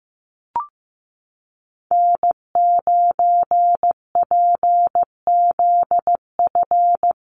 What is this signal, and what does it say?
Morse code, heard as a keyed tone, about 15 wpm, 700 Hz. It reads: N9PZF
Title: USA Call Signs Repeated 3x 15wpm